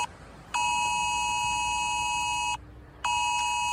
Emergency Alert System (SAME) tones (the 853 and 960 Hz attention tone), with noise. It is heard from 0.5 to 2.6 s.
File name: posts / 24th-april-2024-the-results-of-the-emergency-alert-test-on-bbc-5live